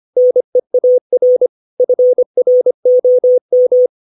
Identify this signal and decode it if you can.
Morse code, heard as a keyed tone, about 25 wpm, 500 Hz. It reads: NEAR FROM
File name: 632 words / 632 Words 25WPM